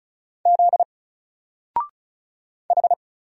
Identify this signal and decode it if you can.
Morse code, heard as a keyed tone, about 36 wpm, 700 Hz. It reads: Z H
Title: Single Letter-Number Alt 35wpm